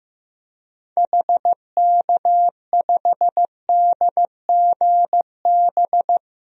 Morse code, heard as a keyed tone, about 15 wpm, 700 Hz. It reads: HK5DGB
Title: DXCC-100 Call Signs - 6 Characters - 15wpm